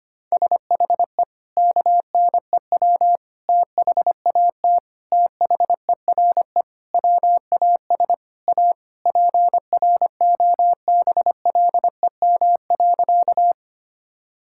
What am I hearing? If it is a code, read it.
Morse code, heard as a keyed tone, about 25 wpm, 700 Hz. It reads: SHE KNEW THAT THERE WAS A PROBLEM.
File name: practice / 25wpm / Sentences with 4-Word Phrases - 25wpm